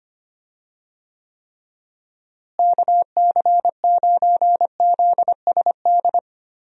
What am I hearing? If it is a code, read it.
Morse code, heard as a keyed tone, about 25 wpm, 700 Hz. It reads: KC9ZSD
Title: USA Call Signs - 6 Characters - Repeated 3x - 25wpm